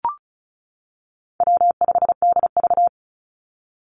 Morse code, heard as a keyed tone, about 36 wpm, 700 Hz. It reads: W5DV
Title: Call Signs - 35wpm